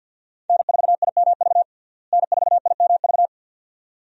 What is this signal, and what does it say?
Morse code, heard as a keyed tone, about 52 wpm, 700 Hz. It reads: N4IMV N4IMV
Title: USA Call Signs - 5 Characters - Repeated 2x - 50wpm